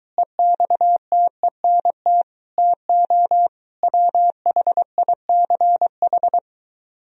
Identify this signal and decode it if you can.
Morse code, heard as a keyed tone, about 23 wpm, 700 Hz. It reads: EXTENT TO WHICH